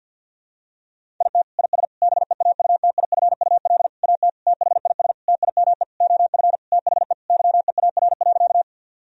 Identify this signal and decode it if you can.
Morse code, heard as a keyed tone, about 51 wpm, 700 Hz. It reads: IT IS BEAUTIFUL AT THIS TIME OF THE YEAR.
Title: Sentences from Top 1000 Words - Review Part 1 - 50wpm